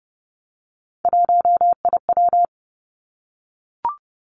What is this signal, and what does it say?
Morse code, heard as a keyed tone, about 30 wpm, 700 Hz. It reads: 1IW
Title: Three Challenging Letter-Number Combinations 30wpm